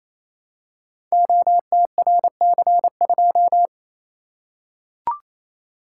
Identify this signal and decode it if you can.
Morse code, heard as a keyed tone, about 29 wpm, 700 Hz. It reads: OTRC2
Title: Code Groups - 5 Letter:Numbers - 28wpm